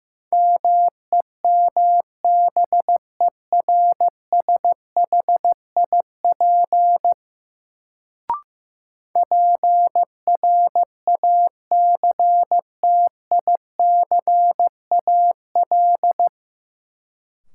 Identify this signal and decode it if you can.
Morse code, heard as a keyed tone, about 15 wpm, 700 Hz. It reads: MEMBERSHIP PRACTICAL
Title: Sentences from Top 1400 Words - New Word Drill - 15wpm